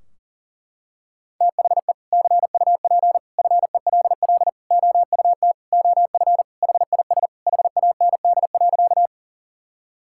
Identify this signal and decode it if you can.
Morse code, heard as a keyed tone, about 40 wpm, 700 Hz. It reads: THE CUP FELL OUT OF HIS HAND.
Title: Sentences from Top 1600 Words 40wpm